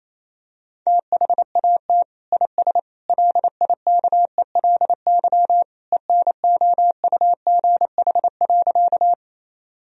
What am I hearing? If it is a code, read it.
Morse code, heard as a keyed tone, about 28 wpm, 700 Hz. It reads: THAT IS LIKELY ENOUGH.
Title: Sentences from Top 1000 Words - Review Part 4 - 28wpm